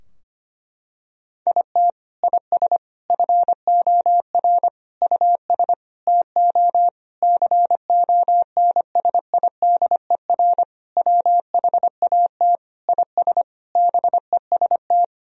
Morse code, heard as a keyed tone, about 25 wpm, 700 Hz. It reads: IT IS FOR US TO CONSIDER WHAT IS BEST
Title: Sentences from Top 1000 Words - Review Part 4 - 25wpm